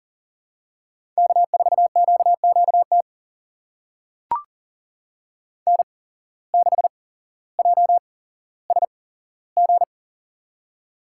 Morse code, heard as a keyed tone, about 40 wpm, spaced as for 17 wpm, 700 Hz. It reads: K4QQT N6JSG
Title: USA Call Signs - 5 Characters - 8x Character Spacing - 40wpm